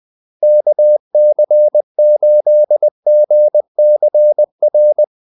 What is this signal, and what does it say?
Morse code, heard as a keyed tone, about 20 wpm, 600 Hz. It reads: KC8GCR
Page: Kaldesignaler 20 wpm